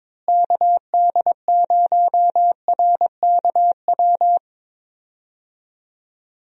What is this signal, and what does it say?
Morse code, heard as a keyed tone, about 22 wpm, 700 Hz. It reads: KD0RKW
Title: USA Call Signs - 6 Characters - Repeated 3x - 22wpm